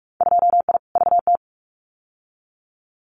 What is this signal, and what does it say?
Morse code, heard as a keyed tone, about 46 wpm, 700 Hz. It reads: 2I VT
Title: Winter Field Day Exchanges 45wpm